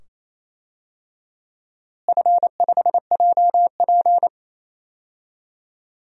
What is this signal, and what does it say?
Morse code, heard as a keyed tone, about 28 wpm, 700 Hz. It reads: F5JP
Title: DXCC-100 Call Signs - 4 Characters - Repeated 2x - 28wpm